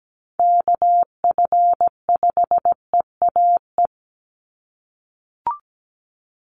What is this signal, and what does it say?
Morse code, heard as a keyed tone, about 17 wpm, 700 Hz. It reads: KF5EAE
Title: USA Call Signs 17wpm